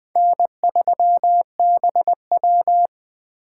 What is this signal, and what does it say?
Morse code, heard as a keyed tone, about 21 wpm, 700 Hz. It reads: N3BW
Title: Speed Racing - USA 1x2 Call Signs 17wpm